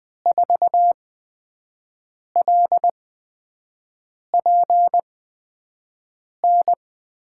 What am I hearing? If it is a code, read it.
Morse code, heard as a keyed tone, about 20 wpm, spaced as for 5 wpm, 700 Hz. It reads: 4LPN
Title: USA Call Signs - 5 Characters - 8x Character Spacing - 20wpm